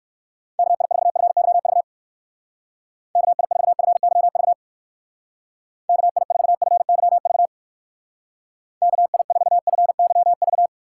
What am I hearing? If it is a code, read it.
Morse code, heard as a keyed tone, about 64 wpm, 700 Hz. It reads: KI4FYV KI4FYV KI4FYV KI4FYV